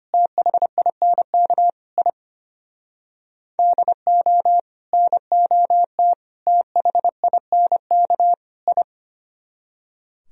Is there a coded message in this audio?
Yes: Morse code from 0.1 to 8.8 s, heard as a keyed tone, about 32 wpm, 700 Hz. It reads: THINK I DO NOT THINK I